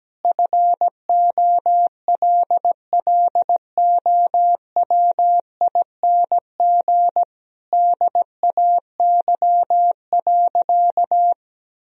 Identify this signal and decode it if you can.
Morse code, heard as a keyed tone, about 17 wpm, 700 Hz. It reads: FOLLOWING DAY.
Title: Sentences from Top 1000 Words - Review Part 4 - 17wpm